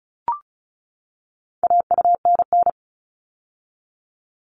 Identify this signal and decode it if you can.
Morse code, heard as a keyed tone, about 34 wpm, 700 Hz. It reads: AUNN
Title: All Cut Numbers - 4-Digit 35wpm